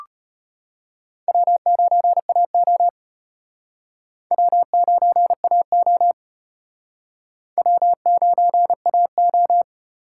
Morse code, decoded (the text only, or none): W9AO W9AO W9AO